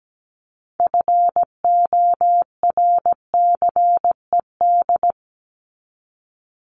Morse code, heard as a keyed tone, about 17 wpm, 700 Hz. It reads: FORCED
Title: ED Suffix - Top 100 Words with Base Words 17wpm